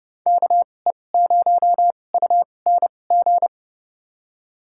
Morse code, heard as a keyed tone, about 30 wpm, spaced as for 22 wpm, 700 Hz. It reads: KE0UNG
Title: USA Call Signs - 6 Characters - 2x Character Spacing - 30wpm